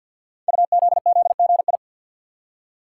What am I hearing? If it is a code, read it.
Morse code, heard as a keyed tone, about 47 wpm, 700 Hz. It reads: UZZGI
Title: Code Groups - 5 Letters - 50wpm